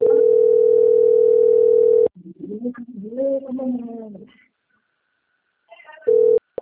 Ringback tone (North America), 440 + 480 Hz.